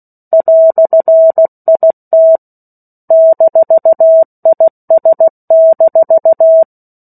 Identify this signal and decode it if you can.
Morse code, heard as a keyed tone, about 16 wpm, 650 Hz. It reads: "IT -IS-
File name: 03-Chapter-03 16 WPM